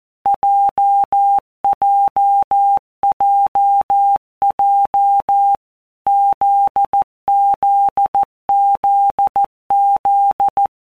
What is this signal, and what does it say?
Morse code, heard as a keyed tone, about 14 wpm, 800 Hz. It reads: JJJJ ZZZZ
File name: Basic Letter Groups By WPM's / Groups@13 / JZ17@13